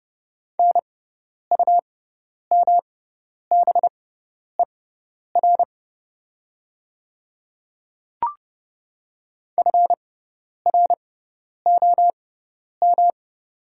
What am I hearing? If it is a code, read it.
Morse code, heard as a keyed tone, about 30 wpm, spaced as for 10 wpm, 700 Hz. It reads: NUMBER FROM
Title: 6x Character Spacing - Top 200 Words - 30wpm